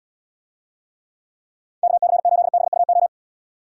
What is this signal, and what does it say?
Morse code, heard as a keyed tone, about 78 wpm, 700 Hz. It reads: KC8BUO